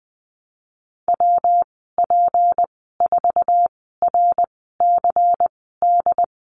Morse code, heard as a keyed tone, about 20 wpm, spaced as for 15 wpm, 700 Hz. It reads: WP4RCD
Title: 167 Lesson 27g - Number 7 - Call Signs 20wpm